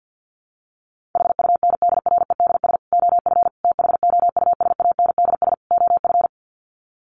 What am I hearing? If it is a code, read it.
Morse code, heard as a keyed tone, about 50 wpm, 700 Hz. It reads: HUNDREDS OF THOUSANDS OF
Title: Sets of 4 Words 50wpm